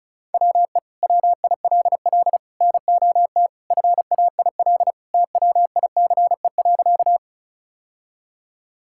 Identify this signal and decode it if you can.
Morse code, heard as a keyed tone, about 36 wpm, 700 Hz. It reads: WE WILL NOT FAIL TWICE.
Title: Sentences from Top 1600 Words 35wpm